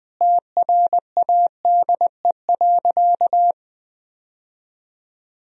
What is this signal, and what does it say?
Morse code, heard as a keyed tone, about 21 wpm, 700 Hz. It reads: TRADE.
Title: Sentences from Top 1000 Words - Review Part 4 - 20wpm